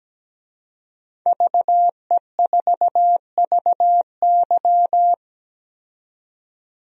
Morse code, heard as a keyed tone, about 17 wpm, 700 Hz. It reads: VE4VY
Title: DXCC-100 Call Signs - 5 Characters - 17wpm